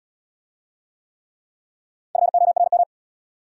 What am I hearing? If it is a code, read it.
Morse code, heard as a keyed tone, about 63 wpm, 700 Hz. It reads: FORM